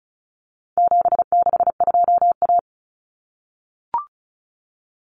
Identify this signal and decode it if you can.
Morse code, heard as a keyed tone, about 34 wpm, 700 Hz. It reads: 762A
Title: Common Cut Numbers - 4-Digit 35wpm